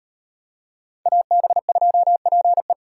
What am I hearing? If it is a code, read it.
Morse code, heard as a keyed tone, about 34 wpm, 700 Hz. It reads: AB2PE